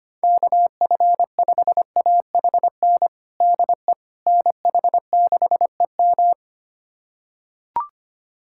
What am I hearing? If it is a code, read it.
Morse code, heard as a keyed tone, about 25 wpm, 700 Hz. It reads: KF5AHN DE NH6EM
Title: USA Callsign DE Callsign 25wpm